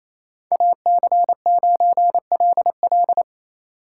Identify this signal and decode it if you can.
Morse code, heard as a keyed tone, about 33 wpm, 700 Hz. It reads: AC9LL
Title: Speed Racing - USA 2x2 Call Signs 28wpm